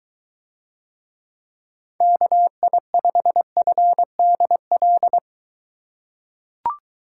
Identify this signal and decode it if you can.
Morse code, heard as a keyed tone, about 23 wpm, 700 Hz. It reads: KI5FDL